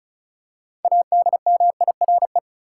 Morse code, heard as a keyed tone, about 36 wpm, 700 Hz. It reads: ADMIRE
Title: Six Letter Words - 35wpm